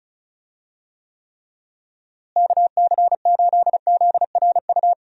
Morse code, heard as a keyed tone, about 34 wpm, 700 Hz. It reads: KC8ZRU
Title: USA Call Signs - 6 Characters - Repeated 3x - 35wpm